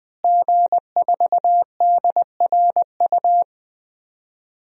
Morse code, heard as a keyed tone, about 20 wpm, 700 Hz. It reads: G4DRU
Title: DXCC-100 Call Signs - 5 Characters - Repeated 3x - 20wpm